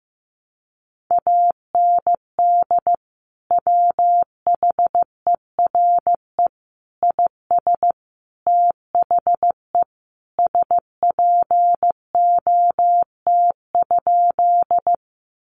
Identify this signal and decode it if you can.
Morse code, heard as a keyed tone, about 15 wpm, 700 Hz. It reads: AND WHERE IS THE SPOT?
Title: Sentences from Top 1100 Words 15wpm